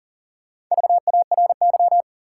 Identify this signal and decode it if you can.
Morse code, heard as a keyed tone, about 40 wpm, 700 Hz. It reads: VARY
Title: Four Letter Words - 40wpm